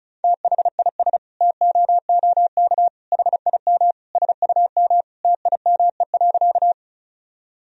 Morse code, heard as a keyed tone, about 34 wpm, 700 Hz. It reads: THIS TOOK HIM SUM TIME.
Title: Sentences from Top 300 Words and Abbreviations 35wpm